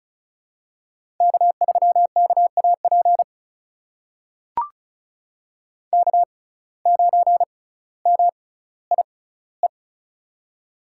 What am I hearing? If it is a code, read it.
Morse code, heard as a keyed tone, about 36 wpm, spaced as for 18 wpm, 700 Hz. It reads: K3KAP K9MIE
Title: USA Call Signs - 5 Characters - 6x Character Spacing - 35wpm